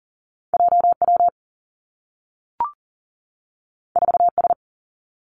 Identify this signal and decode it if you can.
Morse code, heard as a keyed tone, about 38 wpm, 700 Hz. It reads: JW 4S
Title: Two Challenging Letter-Number Combinations 40wpm